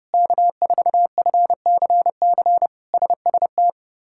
Morse code, heard as a keyed tone, about 30 wpm, 700 Hz. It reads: K4FCC SST
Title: K1USN SST QSO Practice - 30wpm